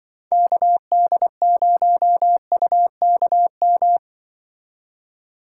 Morse code, heard as a keyed tone, about 23 wpm, 700 Hz. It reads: KD0UKM
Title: Speed Racing - USA 2x3 Call Signs 20wpm